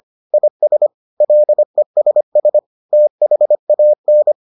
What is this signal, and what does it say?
Morse code, heard as a keyed tone, about 25 wpm, 600 Hz. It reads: IS LESS THAN